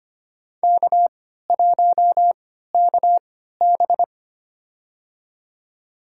Morse code, heard as a keyed tone, about 25 wpm, spaced as for 14 wpm, 700 Hz. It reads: K1KB
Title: USA Call Signs - 4 Characters - 3x Character Spacing - 25wpm